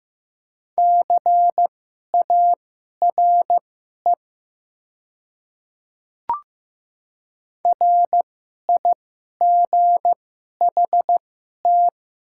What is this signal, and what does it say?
Morse code, heard as a keyed tone, about 15 wpm, spaced as for 11 wpm, 700 Hz. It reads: CARE RIGHT
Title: 2x Character Spacing - Top 300 Words - 15wpm